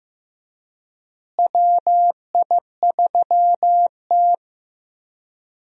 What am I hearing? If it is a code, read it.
Morse code, heard as a keyed tone, about 15 wpm, 700 Hz. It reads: WI3T
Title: USA Call Signs - 4 Characters - Repeated 3x - 15wpm